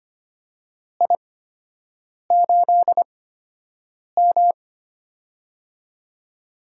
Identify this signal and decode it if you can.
Morse code, heard as a keyed tone, about 25 wpm, spaced as for 7 wpm, 700 Hz. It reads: I8M
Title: USA Call Signs - 4 Characters - 8x Character Spacing - 25wpm